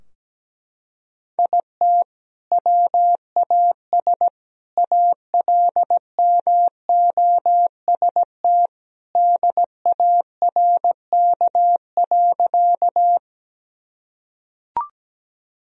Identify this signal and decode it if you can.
Morse code, heard as a keyed tone, about 17 wpm, 700 Hz. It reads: IT WAS ALMOST DARK.
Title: Sentences from Top 1000 Words - Review Part 4 - 17wpm